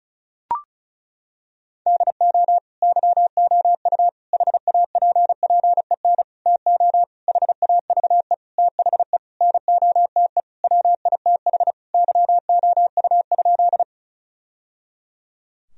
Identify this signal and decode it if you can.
Morse code, heard as a keyed tone, about 34 wpm, 700 Hz. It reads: DO YOU HAPPEN TO HAVE THE NOTE WITH YOU?
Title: Sentences from Top 1000 Words - Review Part 2 - 35wpm